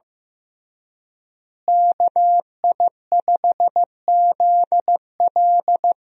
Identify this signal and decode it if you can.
Morse code, heard as a keyed tone, about 15 wpm, 700 Hz. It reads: KI5ZL
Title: USA Call Signs - 5 Characters - Repeated 2x - 15wpm